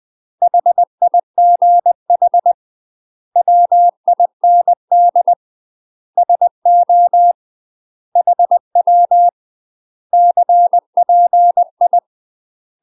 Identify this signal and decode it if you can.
Morse code, heard as a keyed tone, about 20 wpm, 700 Hz. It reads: HIGH WIND SO HW CPI
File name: UK MASTER QSOs TRAINING / 20WPM / UK MASTER QSOs 0067